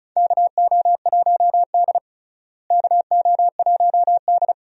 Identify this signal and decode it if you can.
Morse code, heard as a keyed tone, about 34 wpm, 700 Hz. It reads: KO1D KO1D
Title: USA Call Signs - 4 Characters - Repeated 2x - 35wpm